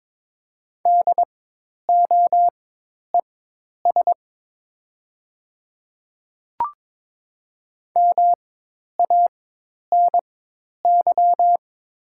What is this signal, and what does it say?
Morse code, heard as a keyed tone, about 22 wpm, spaced as for 10 wpm, 700 Hz. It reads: DOES MANY
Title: 4x Character Spacing - Top 200 Words - 22wpm